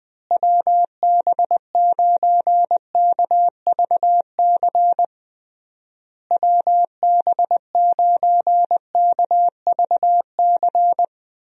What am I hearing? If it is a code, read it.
Morse code, heard as a keyed tone, about 21 wpm, 700 Hz. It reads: WB9KVC WB9KVC
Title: USA Call Signs - 6 Characters - Repeated 3x - 20wpm